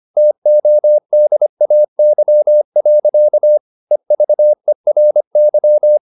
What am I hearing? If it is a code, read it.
Morse code, heard as a keyed tone, about 25 wpm, 600 Hz. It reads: TODAY. EVERY
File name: the old man and the sea part 1 25wpm